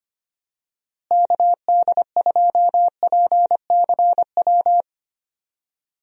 Morse code, heard as a keyed tone, about 25 wpm, 700 Hz. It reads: KD2PCW
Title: USA Call Signs - 6 Characters - Repeated 2x - 25wpm